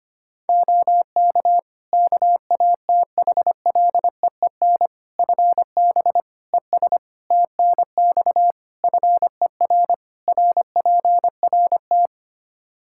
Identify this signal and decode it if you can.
Morse code, heard as a keyed tone, about 25 wpm, 700 Hz. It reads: OK KATHLEEN FB ES TNX FER RPRT
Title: Practice QSO - Exchange 2 - Info - 25wpm